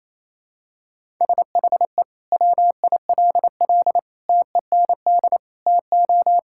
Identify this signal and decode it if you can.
Morse code, heard as a keyed tone, about 28 wpm, 700 Hz. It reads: SHE WILL TEND TO
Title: Sentences from Top 1000 Words - Review Part 5 - 28wpm